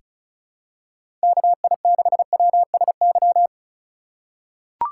Morse code, heard as a keyed tone, about 36 wpm, 700 Hz. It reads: KI6WSY
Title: USA Call Signs - 6 Characters - 2x Character Spacing - 35wpm